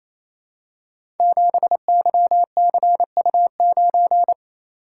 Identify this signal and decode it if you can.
Morse code, heard as a keyed tone, about 28 wpm, 700 Hz. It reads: 7YCU9
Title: Code Groups - 5 Letter:Numbers - 28wpm